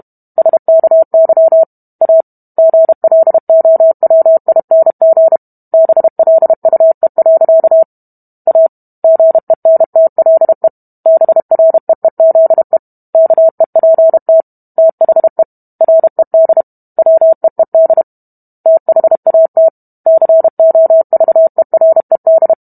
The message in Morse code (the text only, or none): SKY A GLOWING BLUE. A GENTLE BREEZE KEPT THE RED WEED THAT COVERED